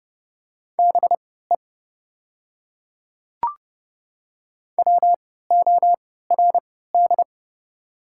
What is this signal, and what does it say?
Morse code, heard as a keyed tone, about 32 wpm, spaced as for 17 wpm, 700 Hz. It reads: BE WORD
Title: 3x Character Spacing - Top 300 Words - 30wpm